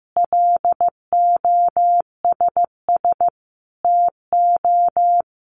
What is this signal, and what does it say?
Morse code, heard as a keyed tone, about 15 wpm, 700 Hz. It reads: LOSS TO
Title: Sentences from Top 1400 Words 15wpm